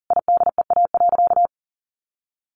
Morse code, heard as a keyed tone, about 40 wpm, 700 Hz. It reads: IDEA.
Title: Sentences from Top 1000 Words 40wpm